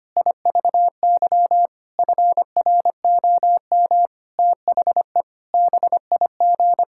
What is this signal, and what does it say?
Morse code, heard as a keyed tone, about 25 wpm, 700 Hz. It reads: IVY FROM THE BIG
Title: Pangram Challenge v2 - 25wpm